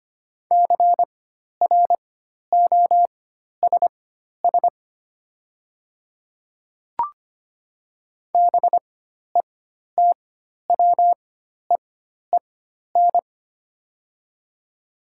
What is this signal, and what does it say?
Morse code, heard as a keyed tone, about 25 wpm, spaced as for 12 wpm, 690 Hz. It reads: CROSS BETWEEN
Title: 4x Character Spacing - Top 300 Words - 25wpm